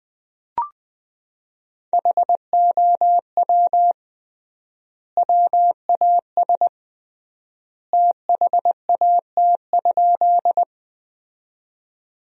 Morse code, heard as a keyed tone, about 20 wpm, 700 Hz. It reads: HOW WAS THAT?
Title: 3x Word-Spacing - Sentences from Top 100 Words - 20wpm